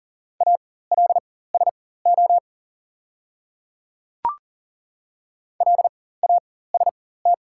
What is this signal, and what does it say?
Morse code, heard as a keyed tone, about 40 wpm, spaced as for 19 wpm, 690 Hz. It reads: ALSO LAST
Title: 4x Character Spacing - Top 300 Words - 40wpm